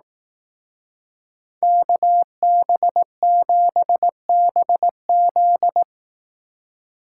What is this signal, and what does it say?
Morse code, heard as a keyed tone, about 18 wpm, 700 Hz. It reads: KB7BZ
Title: Speed Racing - USA 2x2 Call Signs 15wpm